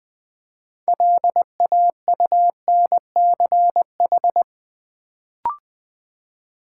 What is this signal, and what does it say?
Morse code, heard as a keyed tone, about 20 wpm, 690 Hz. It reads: LAUNCH
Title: Build A Word - 2 Letters At A Time - 6-Letter Words - 20wpm